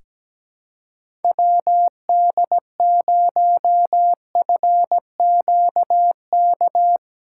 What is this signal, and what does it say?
Morse code, heard as a keyed tone, about 17 wpm, 700 Hz. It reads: WD0FQK
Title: USA Callsign DE Callsign 17wpm